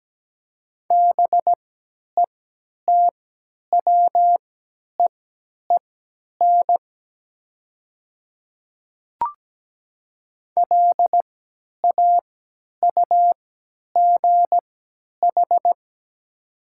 Morse code, heard as a keyed tone, about 17 wpm, spaced as for 10 wpm, 700 Hz. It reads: BETWEEN LAUGH
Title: 3x Character Spacing - Top 500 Words - 17wpm